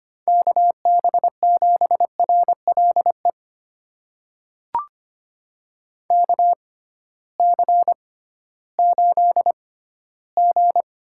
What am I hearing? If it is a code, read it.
Morse code, heard as a keyed tone, about 25 wpm, spaced as for 15 wpm, 700 Hz. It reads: KB7RLE KC8G
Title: USA Call Signs - 6 Characters - 6x Character Spacing - 25wpm